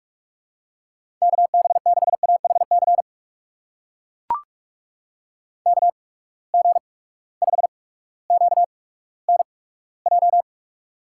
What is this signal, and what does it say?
Morse code, heard as a keyed tone, about 44 wpm, spaced as for 19 wpm, 700 Hz. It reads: KB6AHC KG5QNJ
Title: USA Call Signs - 6 Characters - 8x Character Spacing - 45wpm